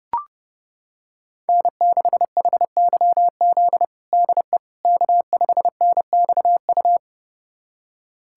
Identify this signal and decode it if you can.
Morse code, heard as a keyed tone, about 30 wpm, 700 Hz. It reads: N6HYZ DE K5NXU
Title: USA Callsign DE Callsign - 5 Characters - 30wpm